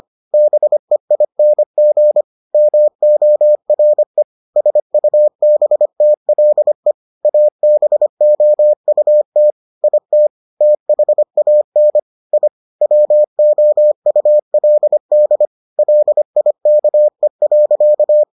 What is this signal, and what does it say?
Morse code, heard as a keyed tone, about 25 wpm, 600 Hz. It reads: BEING MORE SUBTLE ABOUT IT THAN I WOULD LIKE.